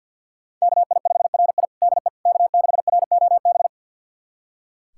Morse code, heard as a keyed tone, about 46 wpm, 700 Hz. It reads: KI5RI DE K6ROB
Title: USA Callsign DE Callsign - 5 Characters - 50wpm